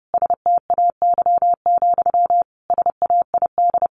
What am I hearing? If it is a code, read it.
Morse code, heard as a keyed tone, about 30 wpm, 700 Hz. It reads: STAY, SAID
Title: Sentences from Top 1500 Words 30wpm